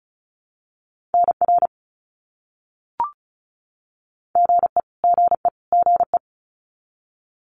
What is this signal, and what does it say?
Morse code, heard as a keyed tone, about 34 wpm, 700 Hz. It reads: NR GE GE GE
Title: Abbreviations Repeated 3x - 35wpm